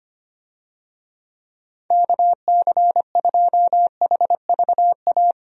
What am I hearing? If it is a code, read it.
Morse code, heard as a keyed tone, about 25 wpm, 700 Hz. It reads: KC2HVA
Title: USA Call Signs - 6 Characters - 25wpm